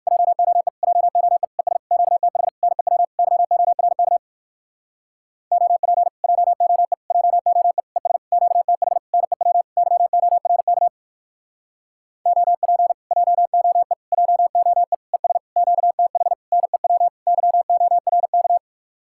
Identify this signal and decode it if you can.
Morse code, heard as a keyed tone, about 67 wpm, spaced as for 59 wpm, 700 Hz. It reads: JOE JOE ES QTH NEW YORK OP JOE JOE ES QTH NEW YORK OP JOE JOE ES QTH NEW YORK